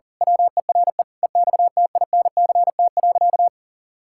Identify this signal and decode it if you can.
Morse code, heard as a keyed tone, about 40 wpm, 700 Hz. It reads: WERE EXTINCT.
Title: Pangram Challenge v2 - 40wpm